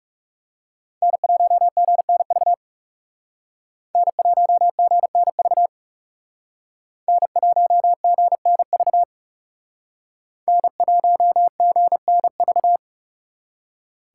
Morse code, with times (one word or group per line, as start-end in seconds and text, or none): 1.0-2.5 N1GNV
4.0-5.7 N1GNV
7.1-9.0 N1GNV
10.5-12.8 N1GNV